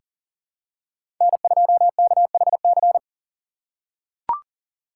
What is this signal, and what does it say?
Morse code, heard as a keyed tone, about 40 wpm, 700 Hz. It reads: N2KHC
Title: USA Call Signs - 5 Characters - 2x Character Spacing - 40wpm